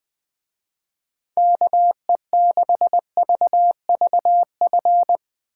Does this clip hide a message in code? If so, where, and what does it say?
Yes: Morse code from 1.4 to 5.2 s, heard as a keyed tone, about 20 wpm, 700 Hz. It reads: KE6VVF